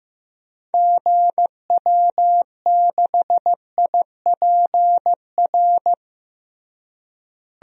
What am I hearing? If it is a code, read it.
Morse code, heard as a keyed tone, about 14 wpm, 700 Hz. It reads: GW6IPR
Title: DXCC-100 Call Signs - 6 Characters - Repeated 3x - 15wpm